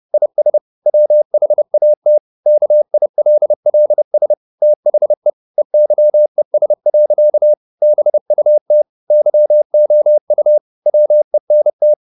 Morse code, heard as a keyed tone, about 30 wpm, 600 Hz. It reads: IS WHAT KILLS THE EYES. BUT YOU WENT
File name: the old man and the sea / the old man and the sea 1 30wpm